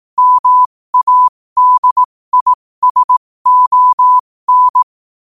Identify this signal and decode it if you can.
Morse code, heard as a keyed tone, about 18 wpm, spaced as for 15 wpm, 1000 Hz. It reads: MADISON
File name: qotdfiles / 15wpm